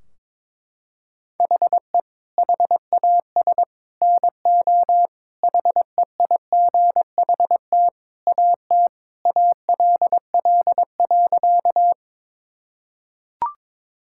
Morse code, heard as a keyed tone, about 21 wpm, 700 Hz. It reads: HE HAS NO HEIGHT AT ALL.
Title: Sentences from Top 1600 Words 22wpm